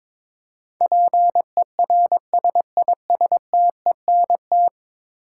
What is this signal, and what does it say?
Morse code, heard as a keyed tone, about 22 wpm, 700 Hz. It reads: PERSISTENT
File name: practice / 22wpm / 10 Letter Words - First 3 Letters Given - 22wpm